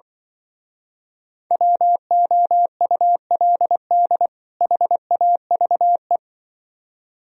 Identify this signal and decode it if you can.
Morse code, heard as a keyed tone, about 25 wpm, 700 Hz. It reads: WOULD HAVE
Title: Speed-Racing - Sets of 2 Words 20wpm